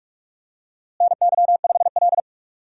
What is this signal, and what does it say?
Morse code, heard as a keyed tone, about 42 wpm, 700 Hz. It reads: NY5L
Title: USA Call Signs - 4 Characters - Repeated 2x - 45wpm